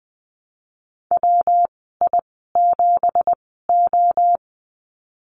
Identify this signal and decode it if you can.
Morse code, heard as a keyed tone, about 20 wpm, spaced as for 15 wpm, 700 Hz. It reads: WI7O
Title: 167 Lesson 27g - Number 7 - Call Signs 20wpm